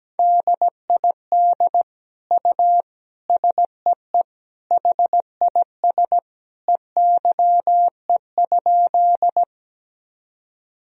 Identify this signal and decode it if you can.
Morse code, heard as a keyed tone, about 17 wpm, 700 Hz. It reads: DID U SEE HIS EYE?
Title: Sentences from Top 300 Words and Abbreviations 17wpm